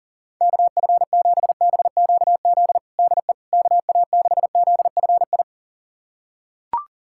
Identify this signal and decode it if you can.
Morse code, heard as a keyed tone, about 37 wpm, 700 Hz. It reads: KF7BQZ DE KA6ZFI
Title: USA Callsign DE Callsign - 6 Characters - 40wpm